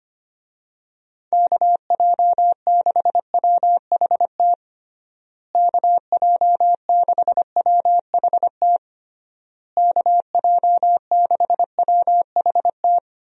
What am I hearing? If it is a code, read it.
Morse code, heard as a keyed tone, about 25 wpm, 700 Hz. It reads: KJ6WHT KJ6WHT KJ6WHT
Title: USA Call Signs - 6 Characters - Repeated 3x - 25wpm